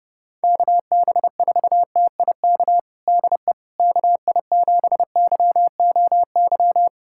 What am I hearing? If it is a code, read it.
Morse code, heard as a keyed tone, about 30 wpm, 700 Hz. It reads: KB4TIK DE KI7YOY
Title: USA Callsign DE Callsign - 6 Characters - 30wpm